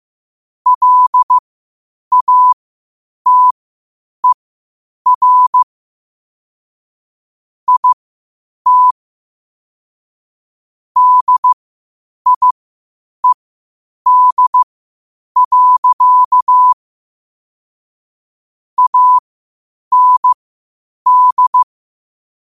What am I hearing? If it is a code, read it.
Morse code, heard as a keyed tone, about 15 wpm, spaced as for 8 wpm, 1000 Hz. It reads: LATER IT DIED. AND
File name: qotdfiles / 8wpm